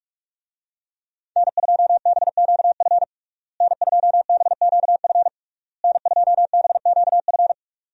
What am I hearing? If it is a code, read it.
Morse code, heard as a keyed tone, about 46 wpm, 700 Hz. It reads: N2BQF N2BQF N2BQF
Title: USA Call Signs - 5 Characters - Repeated 3x - 45wpm